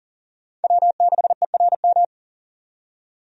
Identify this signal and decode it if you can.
Morse code, heard as a keyed tone, about 40 wpm, 700 Hz. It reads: W6ERM
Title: USA Call Signs - 5 Characters - 40wpm